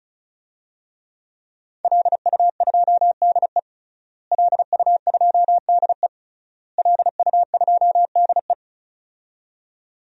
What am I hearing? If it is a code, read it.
Morse code, heard as a keyed tone, about 36 wpm, 700 Hz. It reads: LU2DE LU2DE LU2DE
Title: DXCC-100 Call Signs - 5 Characters - Repeated 3x - 35wpm